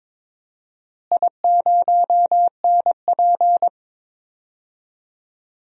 Morse code, heard as a keyed tone, about 22 wpm, 700 Hz. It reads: I0NP
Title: DXCC-100 Call Signs - 4 Characters - Repeated 3x - 22wpm